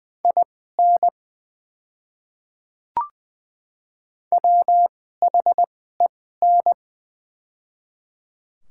Morse code, heard as a keyed tone, about 20 wpm, spaced as for 15 wpm, 700 Hz. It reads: IN WHEN
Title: 2x Character Spacing - Top 100 Words - 20wpm